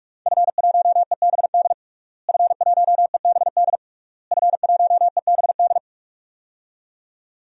Morse code, heard as a keyed tone, about 46 wpm, 700 Hz. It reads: F1EBD F1EBD F1EBD
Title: DXCC-100 Call Signs - 5 Characters - Repeated 3x - 45wpm